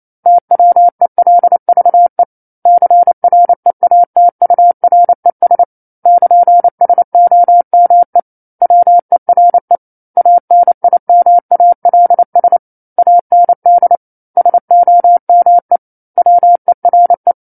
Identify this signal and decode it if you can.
Morse code, heard as a keyed tone, about 28 wpm, 700 Hz. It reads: TWELVE CREATURES <KN> SOME WERE ANIMALS AND SOME WERE
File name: 09-Chapter-09 28 WPM